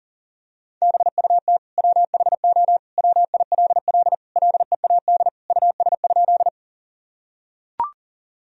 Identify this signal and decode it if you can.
Morse code, heard as a keyed tone, about 40 wpm, 700 Hz. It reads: BUT WHO WILL LEAD US?
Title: Sentences from Top 1000 Words - Review Part 2 - 40wpm